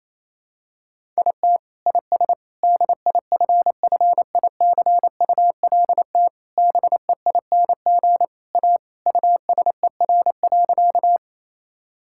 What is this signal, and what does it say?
Morse code, heard as a keyed tone, about 27 wpm, 700 Hz. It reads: IT IS DIFFICULT BEING A USER.
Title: Sentences from Top 1000 Words - Review Part 5 - 28wpm